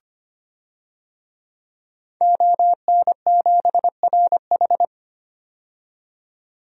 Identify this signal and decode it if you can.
Morse code, heard as a keyed tone, about 25 wpm, 700 Hz. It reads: ON7RH
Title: DXCC-100 Call Signs - 5 Characters - 25wpm